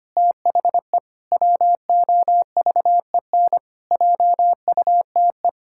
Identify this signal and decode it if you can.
Morse code, heard as a keyed tone, about 25 wpm, 700 Hz. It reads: THE WOVEN JUTE
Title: Pangram Challenge v2 - 25wpm